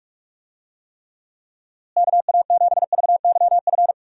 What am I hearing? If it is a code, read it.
Morse code, heard as a keyed tone, about 46 wpm, 700 Hz. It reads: KA7VYF
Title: USA Call Signs - 6 Characters - Repeated 3x - 45wpm